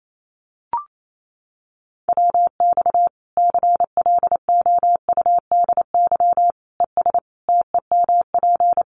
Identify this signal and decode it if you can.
Morse code, heard as a keyed tone, about 28 wpm, 700 Hz. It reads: WX CLOUDY ES TEMP
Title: Common QSO Phrases 28wpm